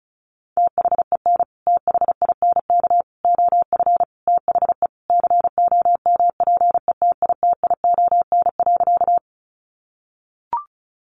Morse code, heard as a keyed tone, about 34 wpm, 700 Hz. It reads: THEN THINK OF THE COMPETITION.
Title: Sentences from Top 1000 Words 35wpm